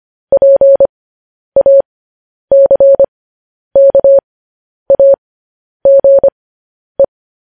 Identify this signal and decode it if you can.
Morse code, heard as a keyed tone, about 25 wpm, spaced as for 10 wpm, 550 Hz. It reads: PACKAGE